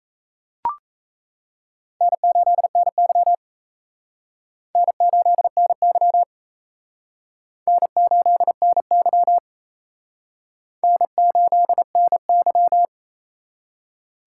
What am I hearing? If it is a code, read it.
Morse code, heard as a keyed tone, about 37 wpm, 700 Hz. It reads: N8NY N8NY N8NY N8NY